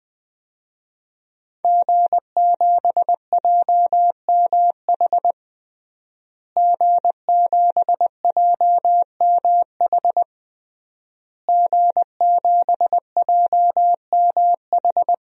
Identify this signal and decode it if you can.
Morse code, heard as a keyed tone, about 20 wpm, 700 Hz. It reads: G7JMH G7JMH G7JMH
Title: DXCC-100 Call Signs - 5 Characters - Repeated 3x - 20wpm